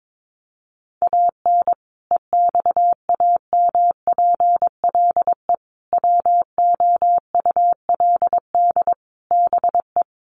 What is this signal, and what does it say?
Morse code, heard as a keyed tone, about 22 wpm, 700 Hz. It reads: AN EXAMPLE WOULD BE
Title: Sentences from Top 400 Words 22wpm